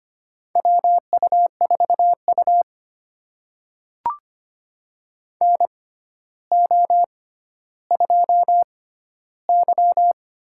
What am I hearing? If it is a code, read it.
Morse code, heard as a keyed tone, about 25 wpm, spaced as for 13 wpm, 700 Hz. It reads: WU4U NO2Y
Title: USA Call Signs - 4 Characters - 6x Character Spacing - 25wpm